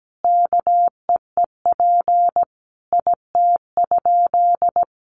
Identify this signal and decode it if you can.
Morse code, heard as a keyed tone, about 17 wpm, 700 Hz. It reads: KEEP IT?
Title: Sentences from Top 300 Words 17wpm